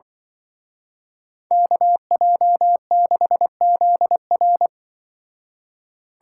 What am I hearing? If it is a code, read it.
Morse code, heard as a keyed tone, about 23 wpm, 700 Hz. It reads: KJ6ZR
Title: Speed Racing - USA Call Signs 20wpm